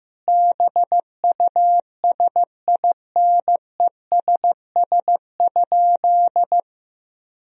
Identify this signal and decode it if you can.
Morse code, heard as a keyed tone, about 15 wpm, 700 Hz. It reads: BUSINESS?
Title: Sentences from Top 1000 Words - Review Part 4 - 15wpm